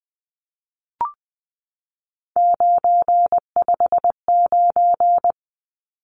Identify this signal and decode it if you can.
Morse code, heard as a keyed tone, about 20 wpm, 700 Hz. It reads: 959
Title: Numbers - 3-Digit 20wpm